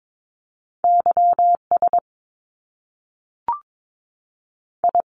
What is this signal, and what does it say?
Morse code, heard as a keyed tone, about 22 wpm, 700 Hz. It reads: Y S I
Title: Top Two Letter Combinations in Words 22wpm